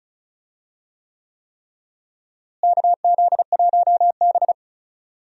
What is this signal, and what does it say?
Morse code, heard as a keyed tone, about 36 wpm, 700 Hz. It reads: KZ1B
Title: USA Call Signs - 4 Characters - Repeated 3x - 35wpm